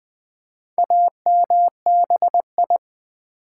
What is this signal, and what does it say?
Morse code, heard as a keyed tone, about 20 wpm, 700 Hz. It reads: AMBI
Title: Build A Word - 2 Letters At A Time - 8-Letter Words - 20wpm